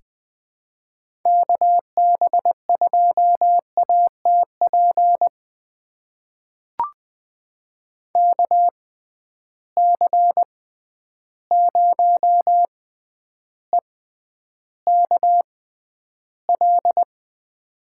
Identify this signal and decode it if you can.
Morse code, heard as a keyed tone, about 20 wpm, spaced as for 10 wpm, 700 Hz. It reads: KB2ATP KC0EKL
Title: USA Call Signs - 6 Characters - 6x Character Spacing - 20wpm